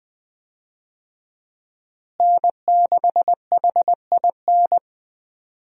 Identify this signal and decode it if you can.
Morse code, heard as a keyed tone, about 21 wpm, 700 Hz. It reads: N6HIN
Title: USA Call Signs - 5 Characters - Repeated 2x - 20wpm